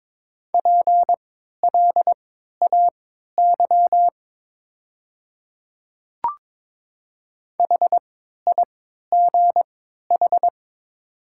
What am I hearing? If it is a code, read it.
Morse code, heard as a keyed tone, about 22 wpm, spaced as for 13 wpm, 700 Hz. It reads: PLAY HIGH
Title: 3x Character Spacing - Top 200 Words - 22wpm